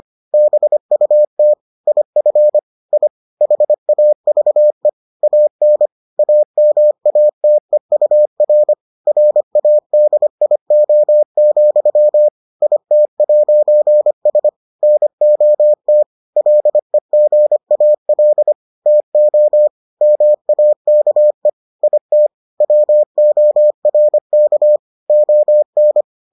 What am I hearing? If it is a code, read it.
Morse code, heard as a keyed tone, about 25 wpm, 600 Hz. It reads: BUT IF I HAVE AN AMATEUR RADIO, IT'S NOT LEGAL TO MAKE IT WORK ON